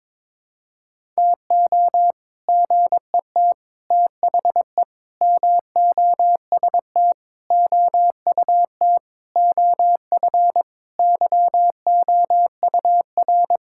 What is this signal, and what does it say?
Morse code, heard as a keyed tone, about 21 wpm, 700 Hz. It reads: TO GET THE MOST OUT OF YOUR
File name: Sentences with 7-Word Phrases - 22wpm